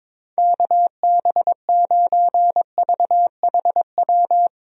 Morse code, heard as a keyed tone, about 22 wpm, 700 Hz. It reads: KB9VHW
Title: USA Call Signs - 6 Characters - Repeated 2x - 22wpm